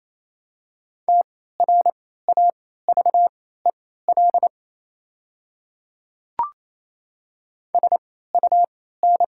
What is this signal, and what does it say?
Morse code, heard as a keyed tone, about 28 wpm, spaced as for 16 wpm, 700 Hz. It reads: TRAVEL SUN
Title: 3x Character Spacing - Top 500 Words - 28wpm